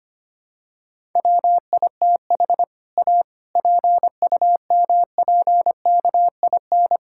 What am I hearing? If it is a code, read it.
Morse code, heard as a keyed tone, about 25 wpm, 700 Hz. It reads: WITH A PUMPKIN
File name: practice / 25wpm / All-Age Appropriate Jokes - 25wpm